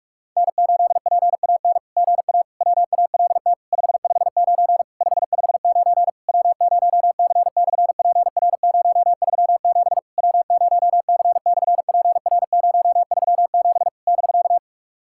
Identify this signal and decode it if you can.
Morse code, heard as a keyed tone, about 44 wpm, 700 Hz. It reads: N8PAN GA WALT 559 559 W0C/PR037 W0C/PR037 <BK>
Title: SOTA QSO Practice 45wpm